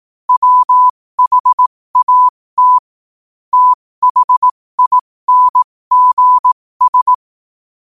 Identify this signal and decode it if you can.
Morse code, heard as a keyed tone, about 18 wpm, spaced as for 15 wpm, 1000 Hz. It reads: WHAT THINGS
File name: 15wpm